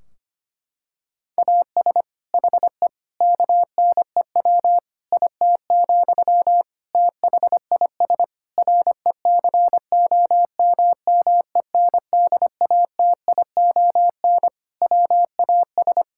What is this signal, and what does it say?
Morse code, heard as a keyed tone, about 25 wpm, 700 Hz. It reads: AS HE KNEW IT, THIS RECOMMENDATION WAS
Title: Sentences from Top 1600 Words 25wpm